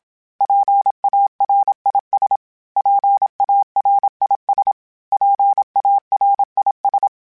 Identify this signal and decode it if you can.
Morse code, heard as a keyed tone, about 27 wpm, 800 Hz. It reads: PARIS PARIS PARIS
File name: PARIS Calibration / PARIS mp3s / PARIS@30